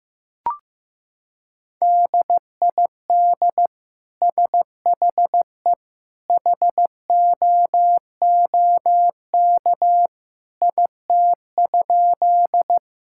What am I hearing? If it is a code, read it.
Morse code, heard as a keyed tone, about 15 wpm, 700 Hz. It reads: DID SHE HOOK IT?
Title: Sentences from Top 1600 Words 15wpm